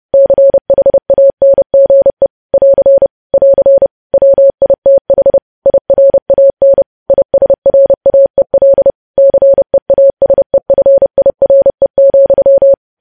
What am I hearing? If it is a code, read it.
Morse code, heard as a keyed tone, about 30 wpm, 550 Hz. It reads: CHANGE <AR> <AR> WITH IRAN ISRAEL CEASEFIRE,
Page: news headlines in morse code at 30 wpm